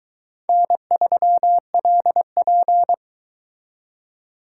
Morse code, heard as a keyed tone, about 22 wpm, 700 Hz. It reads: N3LP